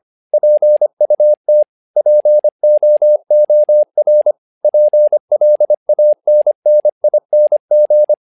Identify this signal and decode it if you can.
Morse code, heard as a keyed tone, about 25 wpm, 600 Hz. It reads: PUT POOR PLANNING
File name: learning center / tower trouble / tower trouble 25wpm